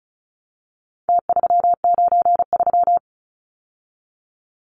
Morse code, heard as a keyed tone, about 34 wpm, 700 Hz. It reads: T393
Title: Numbers - Contest Style - 4-Digit 35wpm